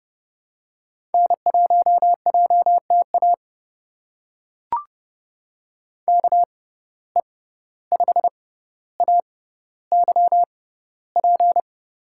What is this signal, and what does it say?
Morse code, heard as a keyed tone, about 30 wpm, spaced as for 15 wpm, 700 Hz. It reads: N1JTA KE5AYP
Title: USA Call Signs - 6x Character Spacing - 30wpm